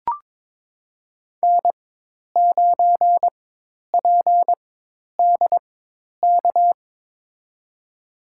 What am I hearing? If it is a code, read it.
Morse code, heard as a keyed tone, about 22 wpm, spaced as for 10 wpm, 700 Hz. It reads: N9PDK
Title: USA Call Signs - 5 Characters - 4x Character Spacing - 22wpm